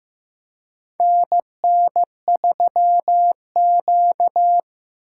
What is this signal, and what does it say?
Morse code, heard as a keyed tone, about 16 wpm, 700 Hz. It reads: NN3Q
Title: USA Call Signs - 4 Characters - 2x Character Spacing - 15wpm